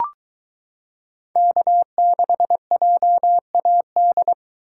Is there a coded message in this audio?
Yes: Morse code from 1.4 to 4.3 s, heard as a keyed tone, about 22 wpm, 700 Hz. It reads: K6JAD